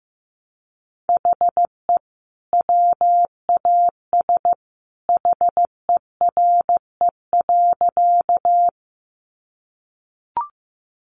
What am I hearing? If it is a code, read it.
Morse code, heard as a keyed tone, about 15 wpm, 700 Hz. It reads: HE WAS HERE.
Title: Sentences from Top 200 Words 15wpm